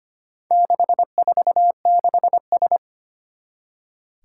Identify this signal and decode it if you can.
Morse code, heard as a keyed tone, about 25 wpm, 700 Hz. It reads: 646S
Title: Four Challenging Letter-Number Combinations 25wpm